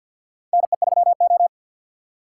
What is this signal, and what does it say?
Morse code, heard as a keyed tone, about 48 wpm, 700 Hz. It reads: NE3O
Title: USA Call Signs - 4 Characters - 3x Character Spacing - 50wpm